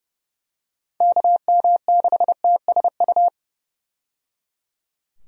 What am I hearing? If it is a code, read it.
Morse code, heard as a keyed tone, about 30 wpm, 700 Hz. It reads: KM6TSU
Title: USA Call Signs - 6 Characters - 30wpm